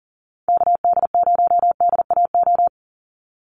Morse code, heard as a keyed tone, about 40 wpm, 700 Hz. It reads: KD0DAO
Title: USA Call Signs 40wpm